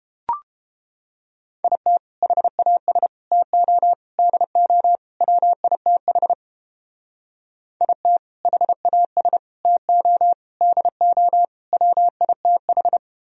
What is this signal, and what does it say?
Morse code, heard as a keyed tone, about 30 wpm, 700 Hz. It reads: IT HAS TO DO WITH IT HAS TO DO WITH